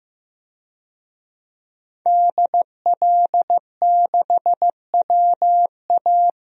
Morse code, heard as a keyed tone, about 15 wpm, 700 Hz. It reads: DL6WA
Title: DXCC-100 Call Signs - 5 Characters - 15wpm